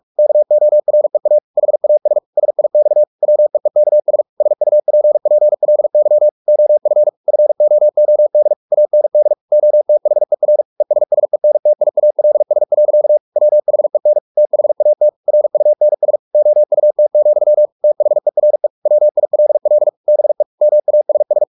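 Morse code, heard as a keyed tone, about 44 wpm, 600 Hz. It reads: KOREA HAS SIX WEEKS SUPPLY OF FOOD AND OTHER ESSENTIALS. WHEN THAT RUNS OUT, THERE WILL BE MASS
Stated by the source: Morse practice at 45 wpm, the novel Never by Ken Follett in English, from CW med Gnister.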